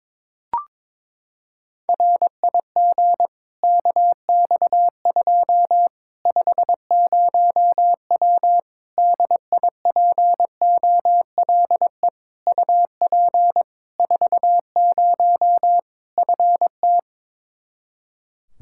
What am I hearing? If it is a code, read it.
Morse code, heard as a keyed tone, about 22 wpm, 700 Hz. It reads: RIG KX2 50W DIPOLE UP 40 FT
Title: Practice QSO - Exchange 2 - Info - 22wpm